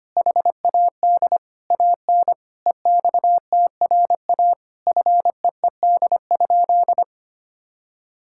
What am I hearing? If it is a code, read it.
Morse code, heard as a keyed tone, about 25 wpm, 700 Hz. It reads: HAD AN EXTRA FEED?
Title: Sentences from Top 1600 Words 25wpm